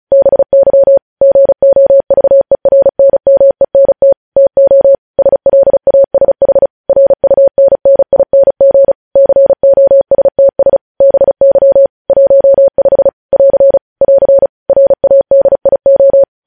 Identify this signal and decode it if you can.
Morse code, heard as a keyed tone, about 36 wpm, 550 Hz. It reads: BY GOVERNMENT TO SLASH RUNNING COSTS BY 15 <AR> <AR> RADIO
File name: podcast / News headlines-1x-35x35